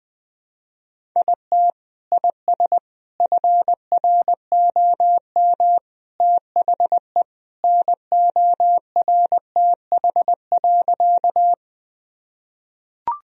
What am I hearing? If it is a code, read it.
Morse code, heard as a keyed tone, about 21 wpm, 700 Hz. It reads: IT IS FROM THE NORTH.
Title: Sentences from Top 1000 Words - Review Part 1 - 20wpm